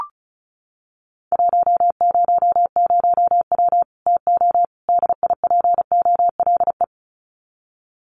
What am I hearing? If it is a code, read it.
Morse code, heard as a keyed tone, about 34 wpm, 700 Hz. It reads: 100W TO DIPOLE
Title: Common QSO Phrases 35wpm